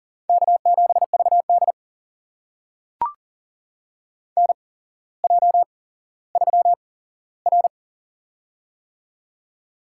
Morse code, heard as a keyed tone, about 40 wpm, spaced as for 17 wpm, 700 Hz. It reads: K7VD NJ3R
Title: USA Call Signs - 4 Characters - 8x Character Spacing - 40wpm